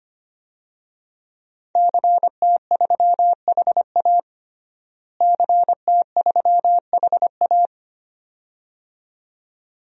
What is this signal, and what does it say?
Morse code, heard as a keyed tone, about 25 wpm, 700 Hz. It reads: CT3HA CT3HA
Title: DXCC-100 Call Signs - 5 Characters - Repeated 2x - 25wpm